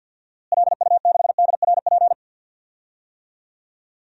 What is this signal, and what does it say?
Morse code, heard as a keyed tone, about 52 wpm, 700 Hz. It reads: LU6DRP
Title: DXCC-100 Call Signs - 6 Characters - Repeated 3x - 50wpm